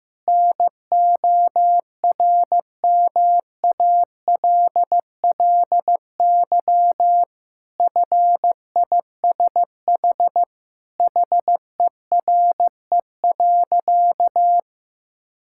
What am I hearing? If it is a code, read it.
Morse code, heard as a keyed tone, about 15 wpm, 700 Hz. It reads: NORMALLY FISH HERE.
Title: Sentences from Top 1000 Words - Review Part 6 - 15wpm